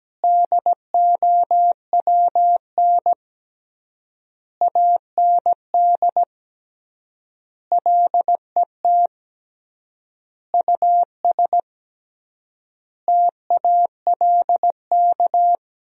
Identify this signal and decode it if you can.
Morse code, heard as a keyed tone, about 17 wpm, 700 Hz. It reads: DOWN AND LET US TALK
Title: 3x Word-Spacing - Sentences from Top 500 Words - 17wpm